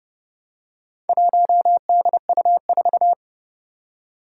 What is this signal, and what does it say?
Morse code, heard as a keyed tone, about 30 wpm, 690 Hz. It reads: 1DU4
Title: Four Challenging Letter-Number Combinations 30wpm